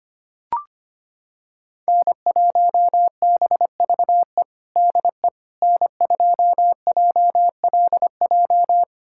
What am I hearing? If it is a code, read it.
Morse code, heard as a keyed tone, about 25 wpm, 700 Hz. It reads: N1BVE DE N2JLJ
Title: USA Callsign DE Callsign - 5 Characters - 25wpm